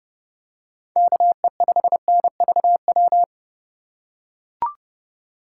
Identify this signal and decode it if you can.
Morse code, heard as a keyed tone, about 30 wpm, 700 Hz. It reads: KE5NVW